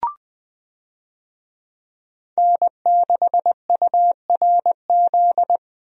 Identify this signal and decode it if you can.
Morse code, heard as a keyed tone, about 20 wpm, 700 Hz. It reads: N6URZ
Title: USA Call Signs - 5 Characters - Repeated 2x - 20wpm